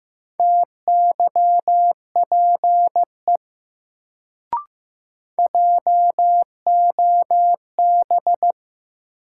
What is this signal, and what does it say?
Morse code, heard as a keyed tone, about 15 wpm, 700 Hz. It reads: TYPE JOB
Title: Sentences from Top 600 Words - New Word Drill - 15wpm